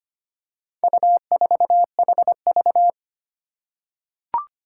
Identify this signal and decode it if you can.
Morse code, heard as a keyed tone, about 25 wpm, 700 Hz. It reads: U4HV
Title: Four Challenging Letter-Number Combinations 25wpm